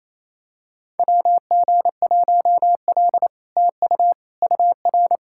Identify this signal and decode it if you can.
Morse code, heard as a keyed tone, about 28 wpm, 700 Hz. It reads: WG1L TU UR
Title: SOTA QSO Practice 28wpm